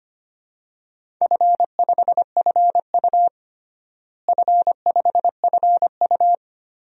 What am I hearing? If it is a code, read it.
Morse code, heard as a keyed tone, about 25 wpm, 700 Hz. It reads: F5FU F5FU
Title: DXCC-100 Call Signs - 4 Characters - Repeated 2x - 25wpm